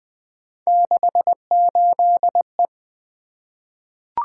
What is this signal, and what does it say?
Morse code, heard as a keyed tone, about 20 wpm, 700 Hz. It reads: 68E
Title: Three Letter-Number Combinations - 20wpm